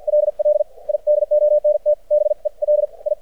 Morse code, heard as a keyed tone, about 50 wpm, spaced as for 40 wpm, 600 Hz. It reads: LL I DONT BELI
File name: archived / CRSnet